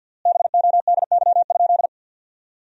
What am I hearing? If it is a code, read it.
Morse code, heard as a keyed tone, about 50 wpm, 700 Hz. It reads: BODY?
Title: 3x Word-Spacing - Sentences from Top 400 Words - 50wpm